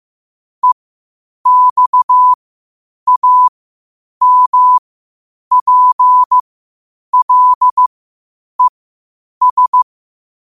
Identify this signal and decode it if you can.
Morse code, heard as a keyed tone, about 15 wpm, spaced as for 8 wpm, 1000 Hz. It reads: EXAMPLES
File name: qotdfiles / 8wpm